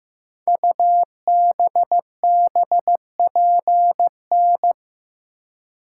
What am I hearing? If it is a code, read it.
Morse code, heard as a keyed tone, about 15 wpm, 700 Hz. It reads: UBBPN
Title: Code Groups - 5 Letter:Numbers - 15wpm